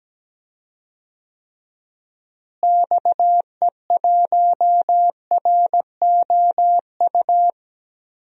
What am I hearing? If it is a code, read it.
Morse code, heard as a keyed tone, about 17 wpm, 700 Hz. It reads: XE1ROU
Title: DXCC-100 Call Signs - 6 Characters - Repeated 3x - 17wpm